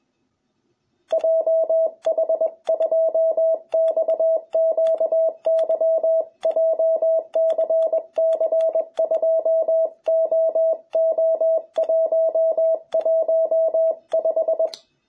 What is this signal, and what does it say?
Morse code, heard as a keyed tone, about 20 wpm, 660 Hz. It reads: JH2XQY JCC2OO115
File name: jh8cbh / 2021cw